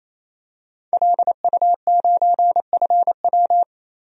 Morse code, heard as a keyed tone, about 28 wpm, 700 Hz. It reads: LU9FW
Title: DXCC-100 Call Signs - 5 Characters - 28wpm